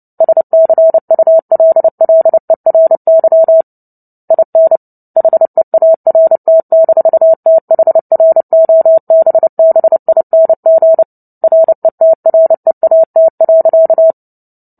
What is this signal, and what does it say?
Morse code, heard as a keyed tone, about 30 wpm, 650 Hz. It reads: SCULLERY IN HEART-THROBBING RETREAT.
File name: Book Two / 03-Chapter 3 - 29 WPM